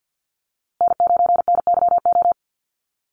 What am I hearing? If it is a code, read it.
Morse code, heard as a keyed tone, about 50 wpm, 700 Hz. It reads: N9NYO
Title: USA Call Signs Repeated 2x 50wpm